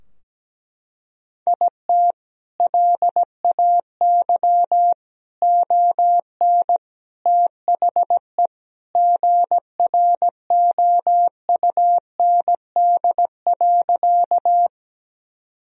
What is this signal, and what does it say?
Morse code, heard as a keyed tone, about 17 wpm, 700 Hz. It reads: IT LAY ON THE GROUND.
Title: Sentences from Top 1000 Words - Review Part 2 - 17wpm